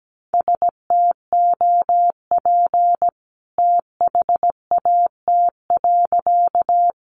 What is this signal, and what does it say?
Morse code, heard as a keyed tone, about 17 wpm, 700 Hz. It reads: STOP THAT.
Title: A Sound Of Thunder 17wpm